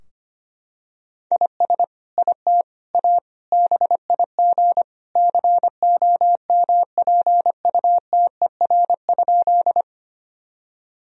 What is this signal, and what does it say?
Morse code, heard as a keyed tone, about 25 wpm, 700 Hz. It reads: IS IT A BIG COMPUTER?
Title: Sentences from Top 1000 Words - Review Part 3 - 25wpm